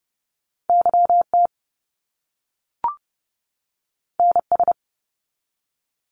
Morse code, heard as a keyed tone, about 30 wpm, 700 Hz. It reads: YT NS
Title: Canadian Provinces and Territories 30wpm